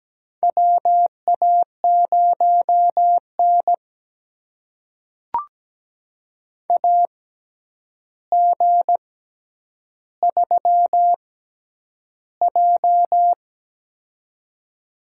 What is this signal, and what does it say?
Morse code, heard as a keyed tone, about 17 wpm, spaced as for 9 wpm, 700 Hz. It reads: WA0N AG3J
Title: USA Call Signs - 4 Characters - 6x Character Spacing - 17wpm